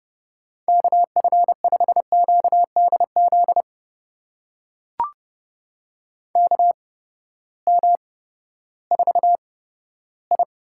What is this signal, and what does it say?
Morse code, heard as a keyed tone, about 30 wpm, spaced as for 15 wpm, 700 Hz. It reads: KF5QDZ KM4I
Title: USA Call Signs - 6 Characters - 8x Character Spacing - 30wpm